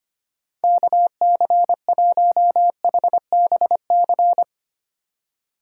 Morse code, heard as a keyed tone, about 32 wpm, 700 Hz. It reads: KC1HBC